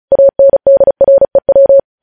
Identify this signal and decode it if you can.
Morse code, heard as a keyed tone, about 36 wpm, 550 Hz. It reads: ANDREW
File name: podcast / News headlines-1x-35x35